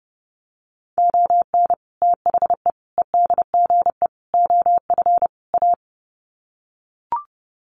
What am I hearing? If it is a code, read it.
Morse code, heard as a keyed tone, about 30 wpm, 700 Hz. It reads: ON THE EDGE OF A
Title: Sets of 5 Words Encore2 30wpm